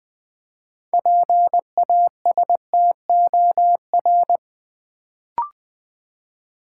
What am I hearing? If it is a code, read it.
Morse code, heard as a keyed tone, about 20 wpm, 700 Hz. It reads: PASTOR
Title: Build A Word - 3 Letters At A Time - 6-Letter Words - 20wpm